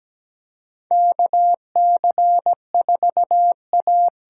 Morse code, heard as a keyed tone, about 17 wpm, 700 Hz. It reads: KC4A
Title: USA Call Signs - 6 Characters - 8x Character Spacing - 17wpm